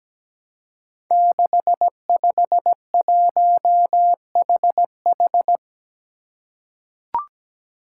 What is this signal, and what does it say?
Morse code, heard as a keyed tone, about 17 wpm, 700 Hz. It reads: 651HH
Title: Five Challenging Letter-Number Combinations 17wpm